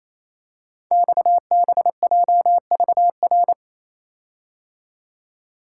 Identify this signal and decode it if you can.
Morse code, heard as a keyed tone, about 27 wpm, 700 Hz. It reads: XBJVR
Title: Code Groups - 5 Letters - 28wpm